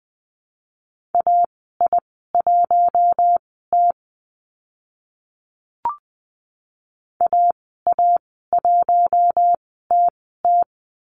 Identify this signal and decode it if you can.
Morse code, heard as a keyed tone, about 20 wpm, spaced as for 14 wpm, 700 Hz. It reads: AI1T AA1TT
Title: 042 Lesson 8g - Number 1 - Call Signs 20wpm